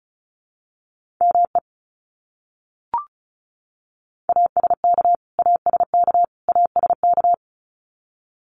Morse code, heard as a keyed tone, about 34 wpm, 700 Hz. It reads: ME ASK ASK ASK
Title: Top 200 Words Repeated 35wpm